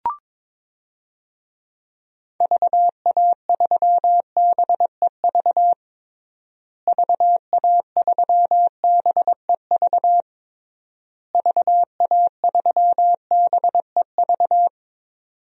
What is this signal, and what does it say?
Morse code, heard as a keyed tone, about 22 wpm, 700 Hz. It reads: VA3BEV VA3BEV VA3BEV
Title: DXCC-100 Call Signs - 6 Characters - Repeated 3x - 22wpm